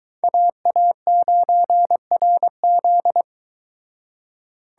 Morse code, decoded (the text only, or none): AA9RZ